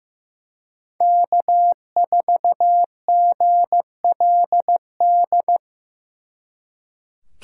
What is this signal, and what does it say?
Morse code, heard as a keyed tone, about 18 wpm, 700 Hz. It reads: K4GLD
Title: Speed Racing - USA 1x3 Call Signs 15wpm